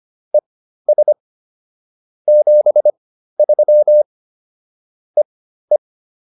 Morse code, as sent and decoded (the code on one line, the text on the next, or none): . ... / --... ...-- / . .
ES 73 EE